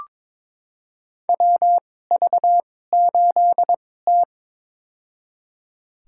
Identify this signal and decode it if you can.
Morse code, heard as a keyed tone, about 22 wpm, spaced as for 16 wpm, 700 Hz. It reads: WV8T
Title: USA Call Signs - 4 Characters - 2x Character Spacing - 22wpm